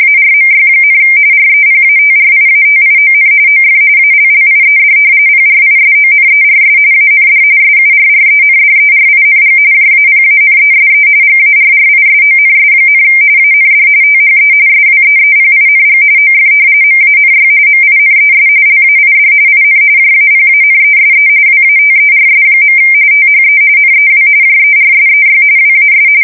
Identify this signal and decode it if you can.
RTTY, about 45.45 baud. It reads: ARMY, REPATRIATION OF ALLIED PRISONERS AND SURRENDERING JAPANESE GARRISONS, AND THE SWEEPING OF MANY MINES LAID IN JAPANESE WATERS. THE NEW JERSEY WAS
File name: fsk170 / battleshipnj